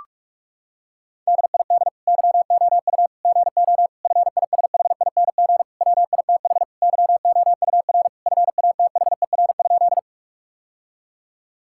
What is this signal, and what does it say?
Morse code, heard as a keyed tone, about 46 wpm, 700 Hz. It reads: DID YOU GO FISHING WITH YOUR FATHER?
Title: Sentences from Top 1000 Words - Review Part 5 - 45wpm